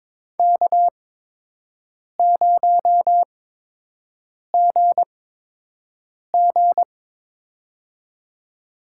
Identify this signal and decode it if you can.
Morse code, heard as a keyed tone, about 22 wpm, spaced as for 6 wpm, 700 Hz. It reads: K0GG
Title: USA Call Signs - 8x Character Spacing - 22wpm